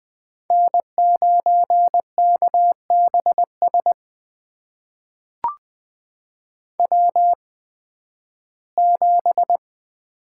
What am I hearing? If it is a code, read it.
Morse code, heard as a keyed tone, about 20 wpm, spaced as for 13 wpm, 700 Hz. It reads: N9KBS W7
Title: USA Call Signs - 5 Characters - 8x Character Spacing - 20wpm